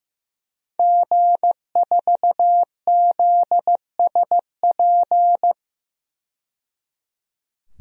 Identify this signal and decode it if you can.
Morse code, heard as a keyed tone, about 15 wpm, 700 Hz. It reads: G4ZSP
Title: DXCC-100 Call Signs - 5 Characters - 15wpm